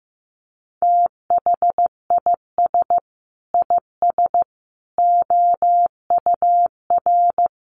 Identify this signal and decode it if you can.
Morse code, heard as a keyed tone, about 15 wpm, 700 Hz. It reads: THIS IS OUR
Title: Sentences from Top 1000 Words 15wpm